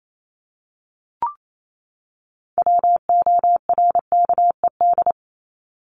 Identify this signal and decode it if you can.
Morse code, heard as a keyed tone, about 28 wpm, 700 Hz. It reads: WORKED
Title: ED Suffix - Top 100 Words with Base Words 28wpm